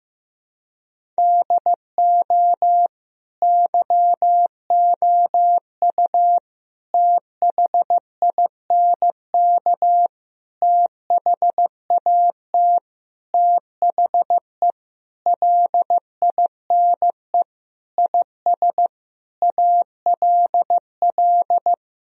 Morse code, heard as a keyed tone, about 15 wpm, 700 Hz. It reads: DO YOU THINK THAT THE LINE IS ALL
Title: Sentences from Top 1000 Words - Review Part 1 - 15wpm